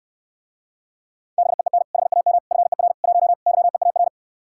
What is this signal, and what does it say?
Morse code, heard as a keyed tone, about 69 wpm, 700 Hz. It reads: BEEN HAM FER 1 YEAR